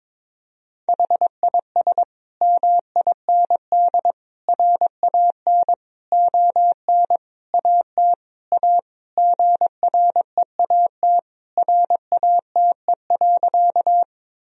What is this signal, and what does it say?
Morse code, heard as a keyed tone, about 22 wpm, 700 Hz. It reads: HIS MIND RAN ON AT A GREAT RATE.
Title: Sentences from Top 1000 Words - Review Part 4 - 22wpm